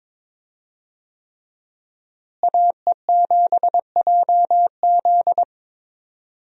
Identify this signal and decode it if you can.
Morse code, heard as a keyed tone, about 22 wpm, 700 Hz. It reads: AE7JZ
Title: USA Call Signs - 5 Characters - Repeated 3x - 22wpm